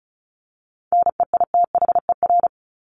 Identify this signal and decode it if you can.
Morse code, heard as a keyed tone, about 34 wpm, 700 Hz. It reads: NEITHER
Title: Top 1501-1750 Words 35wpm